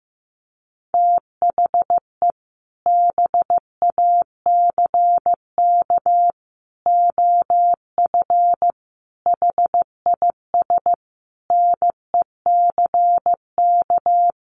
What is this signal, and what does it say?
Morse code, heard as a keyed tone, about 15 wpm, 700 Hz. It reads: THE BACK OF HIS NECK
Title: Sets of 5 Words Encore 15wpm